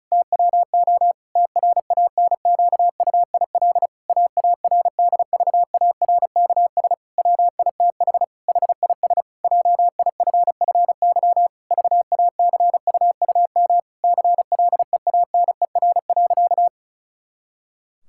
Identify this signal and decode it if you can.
Morse code, heard as a keyed tone, about 36 wpm, 700 Hz. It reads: TWO TRANQUIL AARDVARKS WITH HIS JIFFY VACUUM CLEANER.
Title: Pangram Challenge v2 - 35wpm